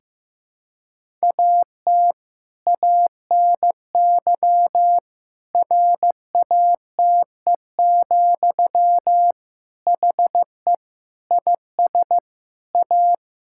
Morse code, heard as a keyed tone, about 15 wpm, 700 Hz. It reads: AT ANY RATE, HE IS A
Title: Sentences from Top 1000 Words - Review Part 5 - 15wpm